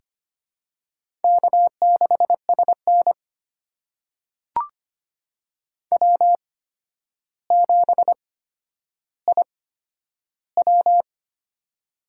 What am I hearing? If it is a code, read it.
Morse code, heard as a keyed tone, about 25 wpm, spaced as for 11 wpm, 700 Hz. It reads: K6SN W7IW
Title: USA Call Signs - 4 Characters - 8x Character Spacing - 25wpm